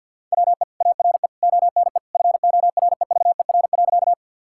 Morse code, heard as a keyed tone, about 50 wpm, 700 Hz. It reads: WE ARE ONE FOREVER.
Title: Sentences from Top 1600 Words 50wpm